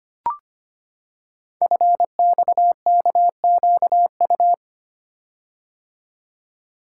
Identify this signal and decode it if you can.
Morse code, heard as a keyed tone, about 25 wpm, 700 Hz. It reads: FXKQU
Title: Code Groups - 5 Letters - 25wpm